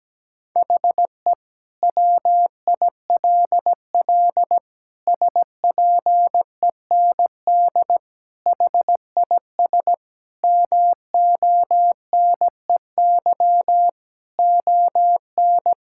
Morse code, heard as a keyed tone, about 17 wpm, 700 Hz. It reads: HE WILL SPEND HIS MONEY ON
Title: Sentences from Top 1000 Words - Review Part 5 - 17wpm